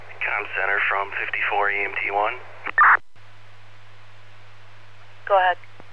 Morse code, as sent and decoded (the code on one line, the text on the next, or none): none